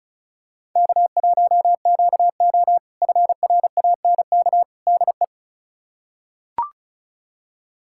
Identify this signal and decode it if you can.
Morse code, heard as a keyed tone, about 36 wpm, 700 Hz. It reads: K1QO FRANK DE
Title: K1USN SST QSO Practice - 35wpm